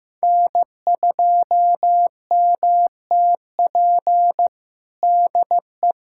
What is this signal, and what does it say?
Morse code, heard as a keyed tone, about 15 wpm, 700 Hz. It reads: N2MTP DE
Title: USA Callsign DE Callsign - 5 Characters - 15wpm